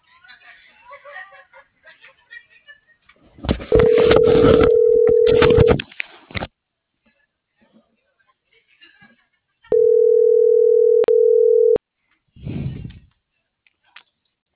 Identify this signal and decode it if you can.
Ringback tone (North America), 440 + 480 Hz.